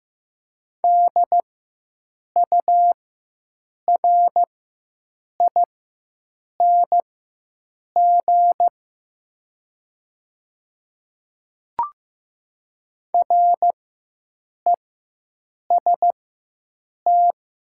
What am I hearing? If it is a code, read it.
Morse code, heard as a keyed tone, about 15 wpm, spaced as for 7 wpm, 700 Hz. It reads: DURING REST
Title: 4x Character Spacing - Top 500 Words - 15wpm